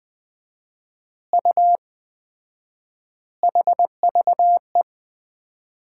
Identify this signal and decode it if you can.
Morse code, heard as a keyed tone, about 20 wpm, 700 Hz. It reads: U HVE
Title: Common QSO Phrases - 4x Word-Spacing - 20wpm